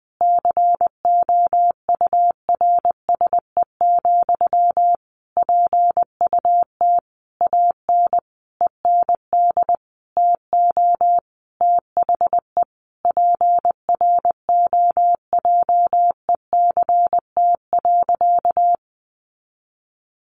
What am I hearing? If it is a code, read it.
Morse code, heard as a keyed tone, about 19 wpm, 700 Hz. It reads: COURSE, PUT AN END TO THE PROJECT.
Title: Sentences from Top 1400 Words 20wpm